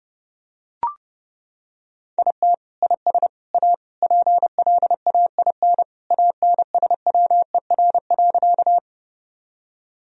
Morse code, heard as a keyed tone, about 30 wpm, 700 Hz. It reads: IT IS A PLAIN ANSWER.
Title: Sentences from Top 1000 Words - Review Part 2 - 30wpm